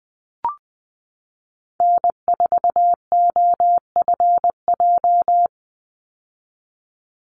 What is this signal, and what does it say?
Morse code, heard as a keyed tone, about 20 wpm, 700 Hz. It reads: N4OFJ
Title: USA Call Signs 20wpm